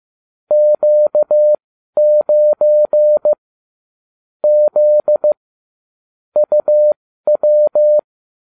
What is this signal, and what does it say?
Morse code, heard as a keyed tone, about 15 wpm, spaced as for 12 wpm, 600 Hz. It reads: Q9 Z UW